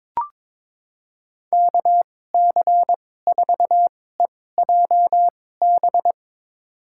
Morse code, heard as a keyed tone, about 22 wpm, spaced as for 16 wpm, 700 Hz. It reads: KC4EJB
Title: USA Call Signs - 6 Characters - 2x Character Spacing - 22wpm